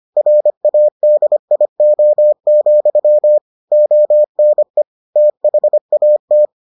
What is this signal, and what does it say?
Morse code, heard as a keyed tone, about 25 wpm, 600 Hz. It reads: RADIO, ONE THAT